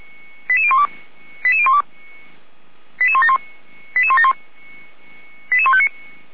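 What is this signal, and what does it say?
Five-tone selective call (ZVEI), heard as quick runs of five tones, tones of about 80 ms. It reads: 88012 88012 88171 88171
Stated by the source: ZVEI five-tone selective call received by SDR.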